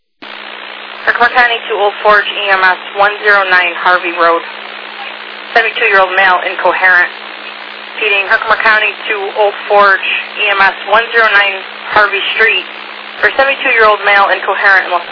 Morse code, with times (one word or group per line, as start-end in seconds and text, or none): none